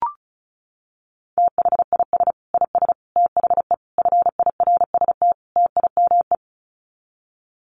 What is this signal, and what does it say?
Morse code, heard as a keyed tone, about 34 wpm, 700 Hz. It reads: THIS IS THE FIRST TIME
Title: Sets of 5 Words 35wpm